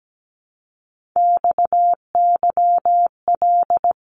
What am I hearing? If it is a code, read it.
Morse code, heard as a keyed tone, about 17 wpm, 700 Hz. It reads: XYL
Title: Abbreviations Repeated 3x - 17wpm